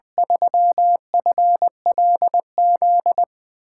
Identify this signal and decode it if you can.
Morse code, heard as a keyed tone, about 20 wpm, 700 Hz. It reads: 3FLZ
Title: USA Call Signs - 8x Character Spacing - 20wpm